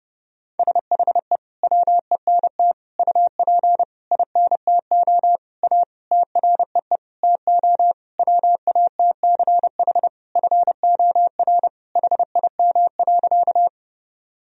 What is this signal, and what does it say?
Morse code, heard as a keyed tone, about 32 wpm, 700 Hz. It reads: SHE WENT UP INTO A TREE TO WATCH FOR HIM.
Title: Sentences from Top 1000 Words - Review Part 1 - 30wpm